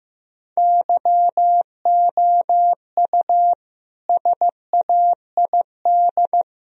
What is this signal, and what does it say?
Morse code, heard as a keyed tone, about 15 wpm, 700 Hz. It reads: YOU SAID
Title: Sentences from Top 1000 Words - Review Part 3 - 15wpm